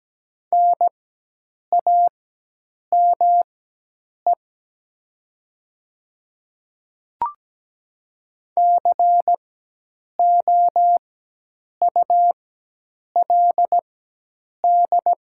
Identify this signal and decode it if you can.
Morse code, heard as a keyed tone, about 17 wpm, spaced as for 8 wpm, 700 Hz. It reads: NAME COULD
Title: 4x Character Spacing - Top 200 Words - 17wpm